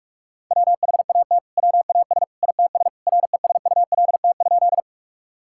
Morse code, heard as a keyed tone, about 41 wpm, 700 Hz. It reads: WHAT WAS ITS RESULT?
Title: Sentences from Top 1000 Words - Review Part 5 - 45wpm